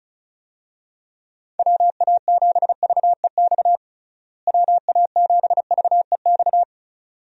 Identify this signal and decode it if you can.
Morse code, heard as a keyed tone, about 36 wpm, 700 Hz. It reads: WA7VEX WA7VEX
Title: USA Call Signs - 6 Characters - Repeated 2x - 35wpm